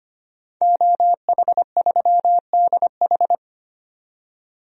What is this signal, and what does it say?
Morse code, heard as a keyed tone, about 25 wpm, 700 Hz. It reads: OH3DH
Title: DXCC-100 Call Signs - 5 Characters - 25wpm